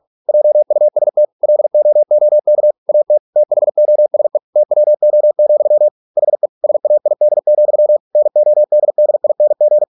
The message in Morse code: JUST LOOK AT THOSE TWO, HE SAID, NODDING

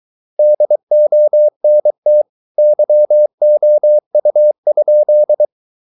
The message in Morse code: DONT YOU?

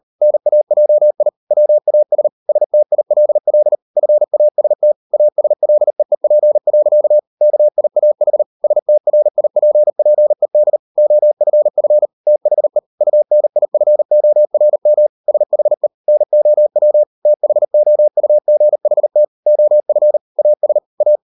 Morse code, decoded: NAJI WAS STILL FAST ASLEEP. KIAH STRIPPED OFF THE UNIFORM SHE NOW THOUGHT OF AS A

